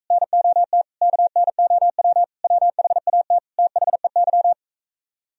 Morse code, heard as a keyed tone, about 44 wpm, 700 Hz. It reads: NOT KNOW WHAT THEY